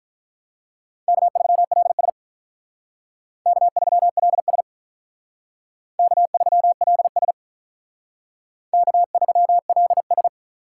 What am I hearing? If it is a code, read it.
Morse code, heard as a keyed tone, about 44 wpm, 700 Hz. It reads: K3LS K3LS K3LS K3LS